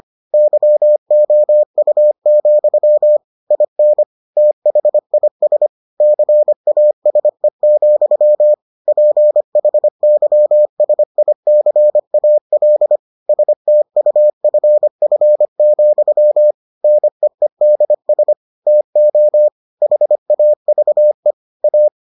Morse code, heard as a keyed tone, about 25 wpm, 600 Hz. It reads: YOU, IN THIS CASE, PHYSICAL STUFF, NEEDS TO HAVE A